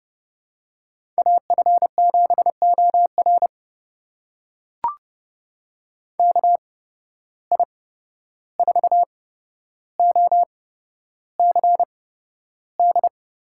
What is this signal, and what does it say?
Morse code, heard as a keyed tone, about 30 wpm, spaced as for 12 wpm, 700 Hz. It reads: AF7OR KI4OCD
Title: USA Call Signs - 8x Character Spacing - 30wpm